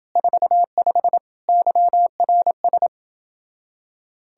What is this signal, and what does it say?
Morse code, heard as a keyed tone, about 29 wpm, 700 Hz. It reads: 45 YRS